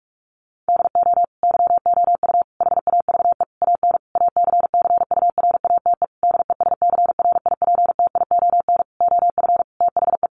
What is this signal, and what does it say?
Morse code, heard as a keyed tone, about 44 wpm, 700 Hz. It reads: DO YOU HAVE AN ACCURATE DESCRIPTION OF THE
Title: Sentences from Top 1300 Words 45wpm